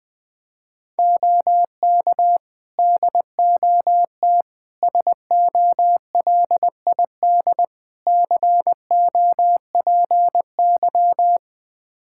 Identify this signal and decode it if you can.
Morse code, heard as a keyed tone, about 21 wpm, 700 Hz. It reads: OK DOT SOLID COPY
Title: Practice QSO - Exchange 3 - More Info - 20wpm